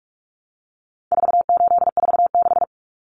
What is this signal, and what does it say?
Morse code, heard as a keyed tone, about 44 wpm, 700 Hz. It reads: 4846
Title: Common Cut Numbers - 4-Digit 45wpm